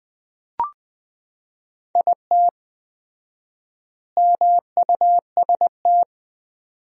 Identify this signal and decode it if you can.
Morse code, heard as a keyed tone, about 21 wpm, 700 Hz. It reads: IT MUST
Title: 4x Word-Spacing - Sentences from Top 500 Words - 20wpm